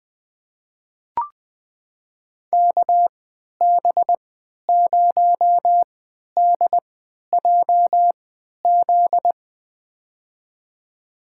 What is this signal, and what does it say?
Morse code, heard as a keyed tone, about 20 wpm, spaced as for 11 wpm, 700 Hz. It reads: KB0DJZ
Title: USA Call Signs - 6 Characters - 3x Character Spacing - 20wpm